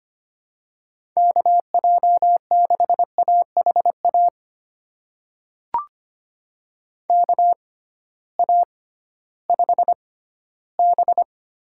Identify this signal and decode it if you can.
Morse code, heard as a keyed tone, about 25 wpm, spaced as for 15 wpm, 700 Hz. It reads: KJ6AHA KA5B
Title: USA Call Signs - 6 Characters - 6x Character Spacing - 25wpm